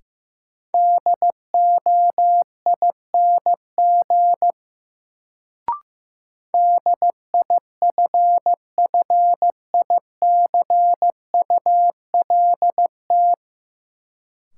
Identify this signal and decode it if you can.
Morse code, heard as a keyed tone, about 15 wpm, 700 Hz. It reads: DOING DIFFICULT
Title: Sets of 4 Words - New Word Drill - 15wpm